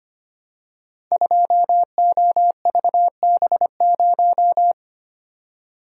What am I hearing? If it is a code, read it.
Morse code, heard as a keyed tone, about 25 wpm, 700 Hz. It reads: 2OVB0
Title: Code Groups - 5 Letter:Numbers - 25wpm